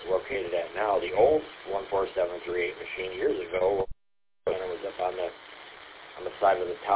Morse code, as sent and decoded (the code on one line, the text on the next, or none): none